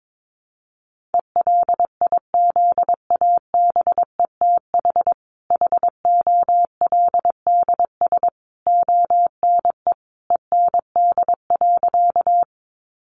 Morse code, heard as a keyed tone, about 22 wpm, 700 Hz. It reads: ELIZABETH HOLDS ONE END.
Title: Easy Phrases 22wpm